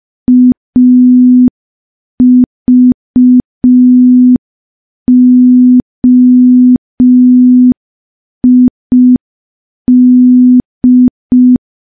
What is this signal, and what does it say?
Morse code, heard as a keyed tone, about 5 wpm, 250 Hz. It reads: AVOID
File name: morse / morse-training-05WPM